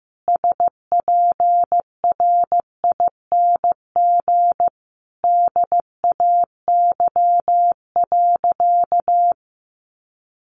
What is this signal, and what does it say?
Morse code, heard as a keyed tone, about 15 wpm, 700 Hz. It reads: SPRING DAY.
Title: Sentences from Top 1200 Words 15wpm